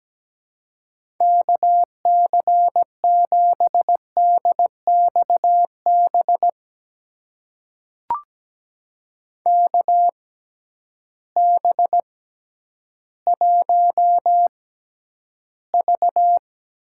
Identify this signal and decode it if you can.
Morse code, heard as a keyed tone, about 17 wpm, spaced as for 10 wpm, 700 Hz. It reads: KC7DXB KB1V
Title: USA Call Signs - 6 Characters - 6x Character Spacing - 17wpm